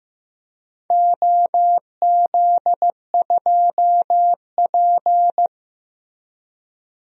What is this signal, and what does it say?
Morse code, heard as a keyed tone, about 16 wpm, 700 Hz. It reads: OZ2P
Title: DXCC-100 Call Signs - 4 Characters - Repeated 3x - 15wpm